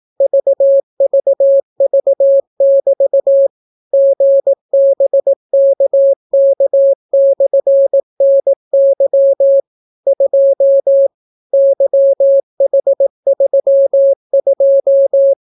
Morse code, heard as a keyed tone, about 18 wpm, 550 Hz. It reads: VVV<BT> GBKK/NY 2 YH32